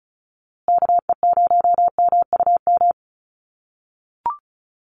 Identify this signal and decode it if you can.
Morse code, heard as a keyed tone, about 34 wpm, 700 Hz. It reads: KE0MUM
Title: USA Call Signs Repeated 3x 35wpm